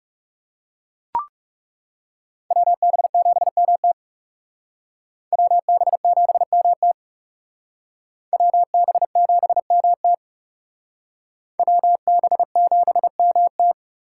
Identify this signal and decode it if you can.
Morse code, heard as a keyed tone, about 37 wpm, 700 Hz. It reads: WB7MT WB7MT WB7MT WB7MT
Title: Speed Racing - USA 2x2 Call Signs 30wpm